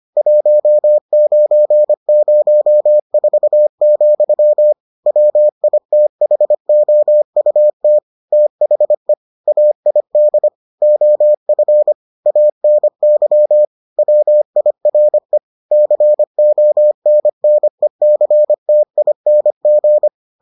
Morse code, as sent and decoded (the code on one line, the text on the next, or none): .---- ----. ----- ....- --..-- / .-- .. - .... --- ..- - / - .... / .- .. -.. / --- ..-. / .- -. -.-- / .-- .. .-. / -.-. --- -. -. -.-. - .. -. --.
1904, WITHOUT THE AID OF ANY WIRE CONNECTING